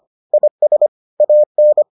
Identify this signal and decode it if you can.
Morse code, heard as a keyed tone, about 25 wpm, 600 Hz. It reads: IS AN